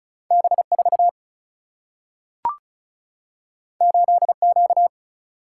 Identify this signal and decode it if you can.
Morse code, heard as a keyed tone, about 36 wpm, 700 Hz. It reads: B4 8Q
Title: Single-Letter Single-Number 35wpm